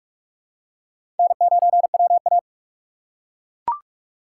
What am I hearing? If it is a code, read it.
Morse code, heard as a keyed tone, about 42 wpm, 700 Hz. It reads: N9WA
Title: USA Call Signs - 4 Characters - 4x Character Spacing - 45wpm